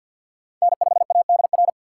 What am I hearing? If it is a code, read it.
Morse code, heard as a keyed tone, about 49 wpm, 700 Hz. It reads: N5ADR
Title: USA Call Signs - 2x Character Spacing - 50wpm